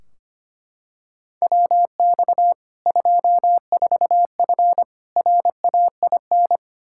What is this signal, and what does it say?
Morse code, heard as a keyed tone, about 25 wpm, 700 Hz. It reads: WX 24F RAIN
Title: Practice QSO - Exchange 2 - Info - 25wpm